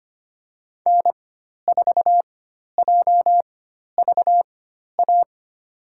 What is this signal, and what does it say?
Morse code, heard as a keyed tone, about 25 wpm, spaced as for 12 wpm, 700 Hz. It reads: N4JVA
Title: USA Call Signs - 5 Characters - 4x Character Spacing - 25wpm